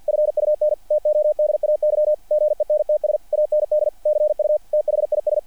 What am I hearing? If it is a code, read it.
Morse code, heard as a keyed tone, about 48 wpm, 660 Hz, with radio noise. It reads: FUN TODAY GENTS AND CU THIS
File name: archived / CRSnet